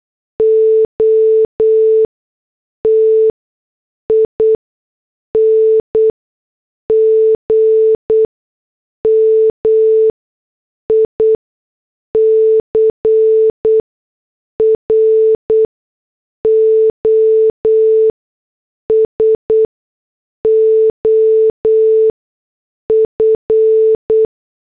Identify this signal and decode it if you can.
Morse code, heard as a keyed tone, about 8 wpm, spaced as for 6 wpm, 440 Hz. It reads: OTING MICROSOF